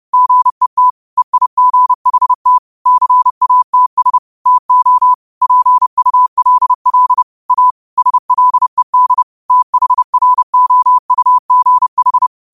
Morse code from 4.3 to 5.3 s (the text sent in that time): TO